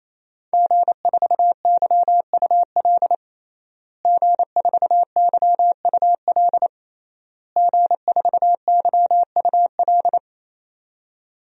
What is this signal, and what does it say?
Morse code, heard as a keyed tone, about 28 wpm, 700 Hz. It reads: G4YUL G4YUL G4YUL
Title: DXCC-100 Call Signs - 5 Characters - Repeated 3x - 28wpm